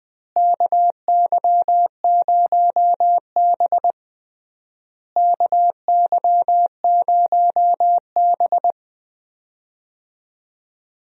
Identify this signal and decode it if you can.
Morse code, heard as a keyed tone, about 20 wpm, 700 Hz. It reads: KY0B KY0B
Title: USA Call Signs - 4 Characters - Repeated 3x - 20wpm